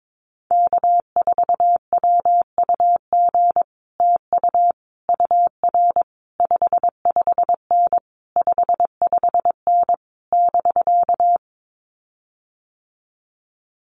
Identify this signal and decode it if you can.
Morse code, heard as a keyed tone, about 22 wpm, 700 Hz. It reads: K4WUG TU UR 55N 55N <BK>
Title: Common QSO Phrases Encore 22wpm